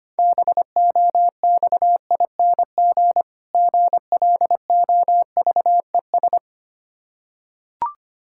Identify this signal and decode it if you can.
Morse code, heard as a keyed tone, about 25 wpm, 700 Hz. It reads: BOXING GLOVES
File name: practice / 25wpm / Difficult Words with Hints - 25wpm